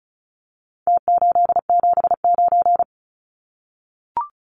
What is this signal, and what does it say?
Morse code, heard as a keyed tone, about 34 wpm, 700 Hz. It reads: T879
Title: Numbers - Contest Style - 4-Digit 35wpm